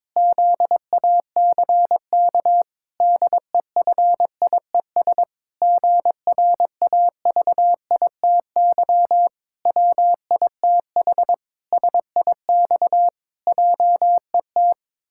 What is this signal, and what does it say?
Morse code, heard as a keyed tone, about 22 wpm, 700 Hz. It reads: ZACK DEFIES GRAVITY WITH SIX JET
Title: Pangram Challenge v2 - 22wpm